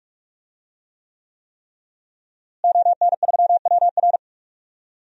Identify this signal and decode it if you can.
Morse code, heard as a keyed tone, about 45 wpm, 700 Hz. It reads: ON3WR
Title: DXCC-100 Call Signs - 5 Characters - Repeated 3x - 45wpm